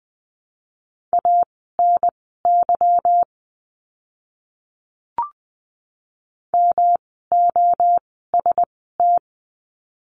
Morse code, heard as a keyed tone, about 20 wpm, spaced as for 15 wpm, 700 Hz. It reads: ANY MOST
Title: 146 Lesson 24f - Letter y - Words 20wpm